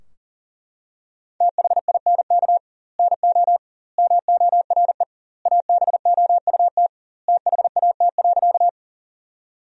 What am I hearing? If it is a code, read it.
Morse code, heard as a keyed tone, about 40 wpm, 700 Hz. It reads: THINK NO MORE ABOUT THAT.
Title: 2x Word-Spacing - Sentences from Top 200 Words - 40wpm